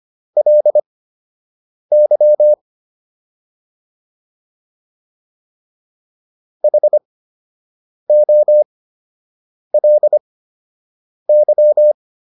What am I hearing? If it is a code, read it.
Morse code, heard as a keyed tone, about 25 wpm, spaced as for 7 wpm, 600 Hz. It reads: LY HOLY
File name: LY4 07